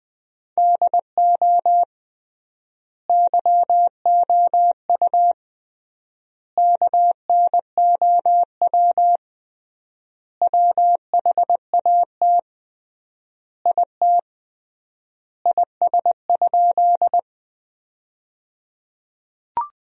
Morse code, heard as a keyed tone, about 20 wpm, 700 Hz. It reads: DO YOU KNOW WHAT IT IS?
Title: 3x Word-Spacing - Sentences from Top 100 Words - 20wpm